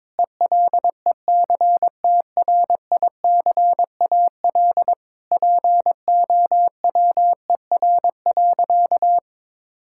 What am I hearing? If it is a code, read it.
Morse code, heard as a keyed tone, about 22 wpm, 700 Hz. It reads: ELECTRICAL POWER.
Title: Sentences from Top 1000 Words - Review Part 6 - 22wpm